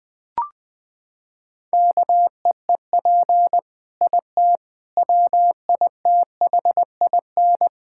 Morse code, heard as a keyed tone, about 19 wpm, 700 Hz. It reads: KEEP IT WITHIN
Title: Sentences from Top 1000 Words - Review Part 6 - 20wpm